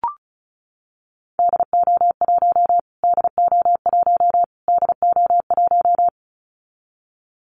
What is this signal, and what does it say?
Morse code, heard as a keyed tone, about 34 wpm, 700 Hz. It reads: DO1 DO1 DO1
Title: Callsign Prefixes for Top 25 DXCC - Repeated 3x - 35wpm